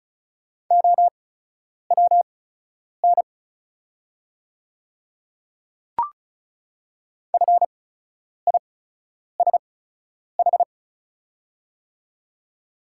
Morse code, heard as a keyed tone, about 36 wpm, spaced as for 10 wpm, 700 Hz. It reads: OWN FISH
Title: 8x Character Spacing - Top 300 Words - 35wpm